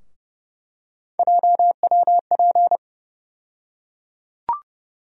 Morse code, heard as a keyed tone, about 30 wpm, 700 Hz. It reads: JWP
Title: USA Call Sign Postfixes - 3 Letters 30wpm